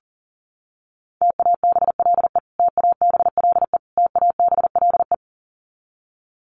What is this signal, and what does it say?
Morse code, heard as a keyed tone, about 40 wpm, 700 Hz. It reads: TABLE TABLE TABLE
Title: Top 500 Words Repeated 40wpm